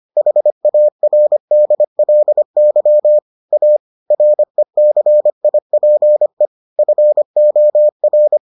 Morse code, heard as a keyed tone, about 25 wpm, 600 Hz. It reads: HARDLY A RECIPE FOR